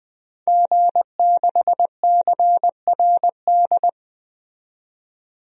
Morse code, heard as a keyed tone, about 20 wpm, 700 Hz. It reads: G6CRD
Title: DXCC-100 Call Signs - 5 Characters - 20wpm